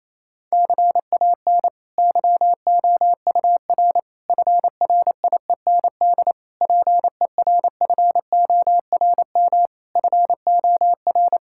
Morse code, heard as a keyed tone, about 28 wpm, 700 Hz. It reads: CAN YOUR FRIEND PERFORM FOR
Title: Sentences from Top 1000 Words - Review Part 5 - 28wpm